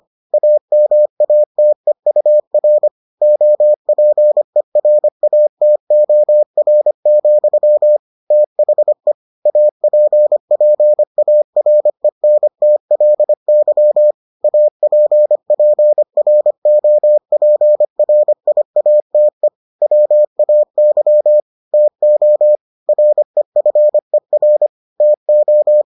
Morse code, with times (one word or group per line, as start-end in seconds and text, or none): 0.3-2.9 AMATEUR
3.2-8.0 OPERATOR,
8.3-9.1 THE
9.5-14.1 APPARENTLY
14.4-19.5 APPROPRIATE
19.8-21.4 WAY
21.7-22.6 TO
22.9-24.7 REFER
25.0-25.8 TO